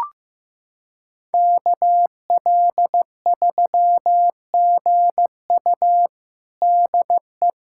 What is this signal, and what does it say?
Morse code, heard as a keyed tone, about 15 wpm, 700 Hz. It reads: KL3GU DE
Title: USA Callsign DE Callsign - 5 Characters - 15wpm